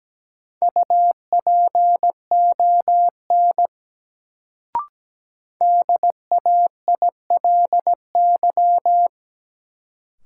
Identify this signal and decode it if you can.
Morse code, heard as a keyed tone, about 17 wpm, 700 Hz. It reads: UPON DAILY
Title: Sentences from Top 1100 Words - New Word Drill Alt - 17wpm